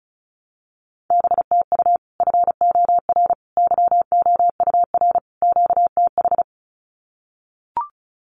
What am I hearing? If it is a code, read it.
Morse code, heard as a keyed tone, about 34 wpm, 700 Hz. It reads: BTU FOR YOUR QTH
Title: Common QSO Phrases 35wpm